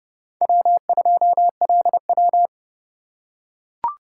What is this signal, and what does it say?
Morse code, heard as a keyed tone, about 29 wpm, 700 Hz. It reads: W2LW
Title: USA Call Signs - 3x Character Spacing - 30wpm